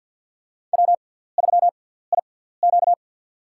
Morse code, heard as a keyed tone, about 46 wpm, spaced as for 17 wpm, 700 Hz. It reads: W3IQ
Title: USA Call Signs - 4 Characters - 6x Character Spacing - 50wpm